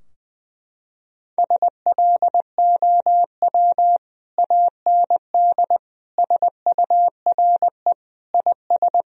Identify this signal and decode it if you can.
Morse code, heard as a keyed tone, about 20 wpm, 700 Hz. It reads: SLOW AND SURE IS
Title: Sentences from Top 1000 Words - Review Part 2 - 20wpm